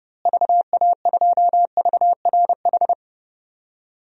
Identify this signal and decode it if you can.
Morse code, heard as a keyed tone, about 30 wpm, 700 Hz. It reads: VA2VRH
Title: DXCC-100 Call Signs - 6 Characters - Repeated 3x - 30wpm